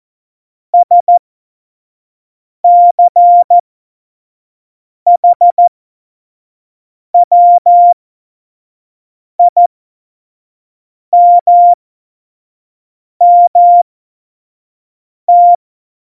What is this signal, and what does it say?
Morse code, heard as a keyed tone, about 14 wpm, spaced as for 5 wpm, 700 Hz. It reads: SCHWIMMT
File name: Morsekurs / Witze / Witz 17.Dezember 2019, 25 F70